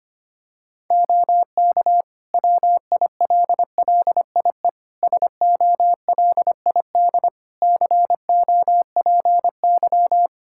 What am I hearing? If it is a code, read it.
Morse code, heard as a keyed tone, about 25 wpm, 700 Hz. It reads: OK WILLIE SOLID COPY
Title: Practice QSO - Exchange 3 - More Info - 25wpm